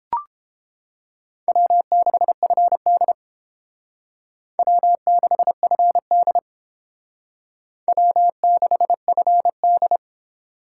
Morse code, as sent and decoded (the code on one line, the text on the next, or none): .-- -.... ..-. -.. / .-- -.... ..-. -.. / .-- -.... ..-. -..
W6FD W6FD W6FD